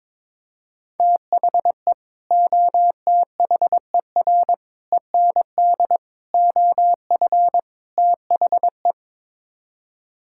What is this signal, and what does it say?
Morse code, heard as a keyed tone, about 22 wpm, 700 Hz. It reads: THE OTHER END OF THE
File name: practice / 22wpm / Build A Phrase - 5-Word Phrases - 22wpm